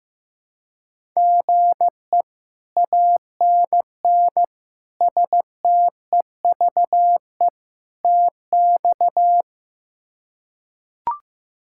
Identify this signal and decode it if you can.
Morse code, heard as a keyed tone, about 15 wpm, 700 Hz. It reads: GE ANN STEVE TX
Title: K1USN SST QSO Practice - 15wpm